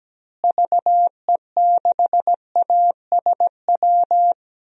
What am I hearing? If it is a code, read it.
Morse code, heard as a keyed tone, about 17 wpm, 700 Hz. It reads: VE6ASW
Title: DXCC-100 Call Signs - 6 Characters - Repeated 2x - 17wpm